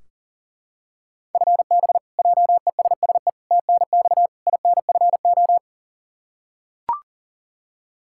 Morse code, heard as a keyed tone, about 40 wpm, 700 Hz. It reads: FB JESSE TNX INFO
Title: Practice QSO - Exchange 3 - More Info - 40wpm